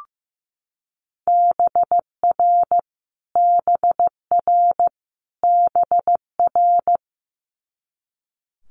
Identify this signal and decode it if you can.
Morse code, heard as a keyed tone, about 15 wpm, 700 Hz. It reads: BR BR BR
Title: Multi-letter Phonemes Repeated 3x 15wpm